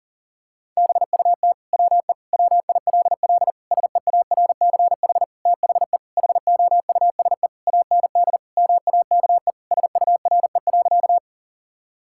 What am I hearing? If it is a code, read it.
Morse code, heard as a keyed tone, about 40 wpm, 700 Hz. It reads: BUT WE WILL SEARCH THE HOUSE AND MAKE SURE.
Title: Sentences from Top 1000 Words - Review Part 5 - 40wpm